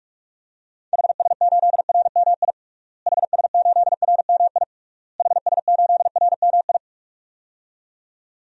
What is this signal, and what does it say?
Morse code, heard as a keyed tone, about 46 wpm, 700 Hz. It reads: HS8RMI HS8RMI HS8RMI
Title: DXCC-100 Call Signs - 6 Characters - Repeated 3x - 45wpm